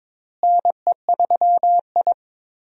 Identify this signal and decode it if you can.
Morse code, heard as a keyed tone, about 22 wpm, 700 Hz. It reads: NE3I
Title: USA Call Signs - 4 Characters - Repeated 3x - 22wpm